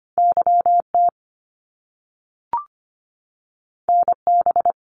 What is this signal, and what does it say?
Morse code, heard as a keyed tone, about 25 wpm, 700 Hz. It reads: YT NB
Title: Canadian Provinces and Territories 25wpm